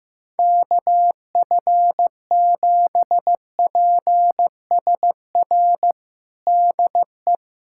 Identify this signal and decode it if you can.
Morse code, heard as a keyed tone, about 15 wpm, 700 Hz. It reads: KF7PSR DE
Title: USA Callsign DE Callsign 15wpm